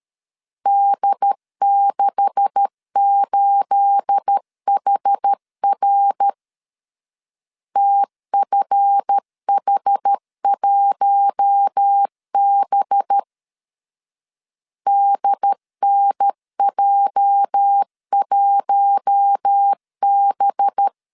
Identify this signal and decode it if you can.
Morse code, heard as a keyed tone, about 13 wpm, 800 Hz. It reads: D68HR TFH1B DNJ1B
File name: CW / DET / 1 9 1500C S C 12wpm 3811